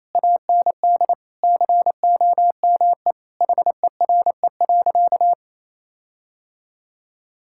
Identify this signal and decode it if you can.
Morse code, heard as a keyed tone, about 28 wpm, 700 Hz. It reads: AND COME HERE.
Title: Sentences from Top 1600 Words 28wpm